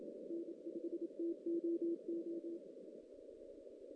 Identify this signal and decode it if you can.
Morse code, heard as a keyed tone, about 27 wpm, 340 Hz. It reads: N5TOO
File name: 2016 JAN CW NAQP / 2016 Jan naqp k5ot - 0-5